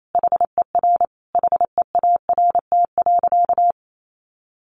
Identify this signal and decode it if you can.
Morse code, heard as a keyed tone, about 28 wpm, 700 Hz. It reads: HER HEART.
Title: Sentences from Top 1200 Words 28wpm